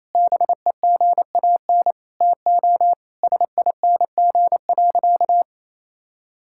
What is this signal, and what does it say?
Morse code, heard as a keyed tone, about 28 wpm, 700 Hz. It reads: BEGAN TO SING.
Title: Sentences from Top 1600 Words 28wpm